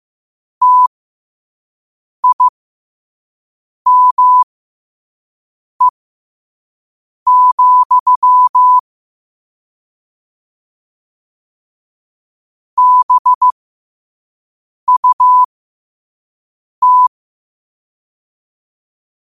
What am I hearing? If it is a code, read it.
Morse code, heard as a keyed tone, about 15 wpm, spaced as for 5 wpm, 1000 Hz. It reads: TIME, BUT
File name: qotdfiles / 5wpm